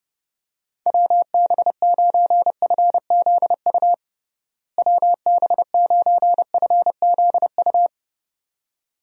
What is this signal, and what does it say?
Morse code, heard as a keyed tone, about 30 wpm, 700 Hz. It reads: WB9FZU WB9FZU
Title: USA Call Signs - 6 Characters - Repeated 2x - 30wpm